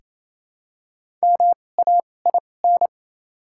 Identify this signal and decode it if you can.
Morse code, heard as a keyed tone, about 28 wpm, spaced as for 20 wpm, 700 Hz. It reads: MAIN
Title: 2x Character Spacing - Top 500 Words - 28wpm